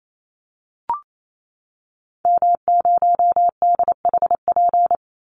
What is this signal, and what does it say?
Morse code, heard as a keyed tone, about 28 wpm, 700 Hz. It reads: M0DHP
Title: Call Signs Repeated 2x - 28wpm